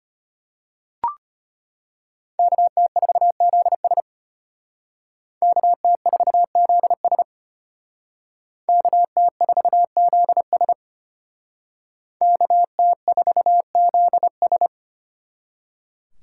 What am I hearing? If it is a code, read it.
Morse code, heard as a keyed tone, about 33 wpm, 700 Hz. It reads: KT4ZS KT4ZS KT4ZS KT4ZS